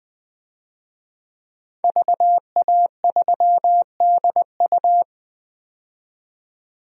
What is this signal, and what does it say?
Morse code, heard as a keyed tone, about 21 wpm, 700 Hz. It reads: VA3DU
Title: DXCC-100 Call Signs - 5 Characters - 20wpm